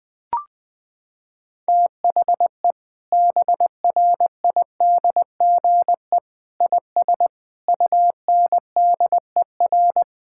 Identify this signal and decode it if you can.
Morse code, heard as a keyed tone, about 21 wpm, 700 Hz. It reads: THE BRIDGE IS UNDER
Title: Sentences from Top 1600 Words 20wpm